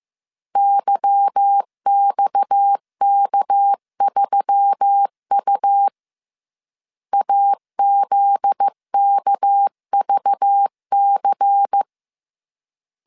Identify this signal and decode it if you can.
Morse code, heard as a keyed tone, about 15 wpm, 800 Hz. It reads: YXK3U AZKVC
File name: CW / DET / 1 9 1500C S C 14wpm 3245